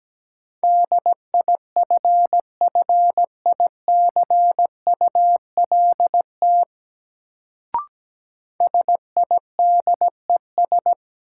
Morse code, heard as a keyed tone, about 17 wpm, 700 Hz. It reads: DIFFICULT SIDES
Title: Sets of 4 Words - New Word Drill - 17wpm